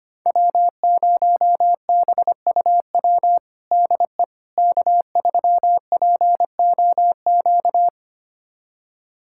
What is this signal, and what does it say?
Morse code, heard as a keyed tone, about 25 wpm, 700 Hz. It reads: W0BUW DE K3POQ
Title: USA Callsign DE Callsign - 5 Characters - 25wpm